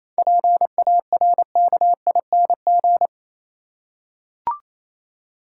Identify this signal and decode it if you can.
Morse code, heard as a keyed tone, about 28 wpm, 700 Hz. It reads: PARKING
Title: Seven Letter Words - 28wpm